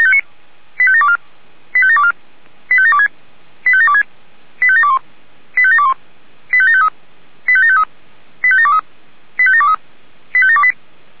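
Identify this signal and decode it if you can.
Five-tone selective call (ZVEI), heard as quick runs of five tones, tones of about 70 ms. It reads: 85724 85724 85726 85726 85721 85721 85753 85753 85723 85723 85728
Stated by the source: ZVEI five-tone selective call 85729, received by SDR.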